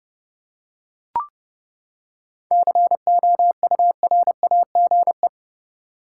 Morse code, heard as a keyed tone, about 31 wpm, 700 Hz. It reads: COURAGE
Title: Seven Letter Words - 30wpm